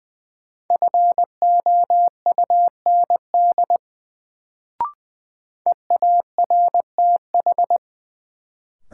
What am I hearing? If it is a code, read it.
Morse code, heard as a keyed tone, about 20 wpm, 700 Hz. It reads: FOUND EARTH
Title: Sets of 5 Words Encore - New Word Drill - 20wpm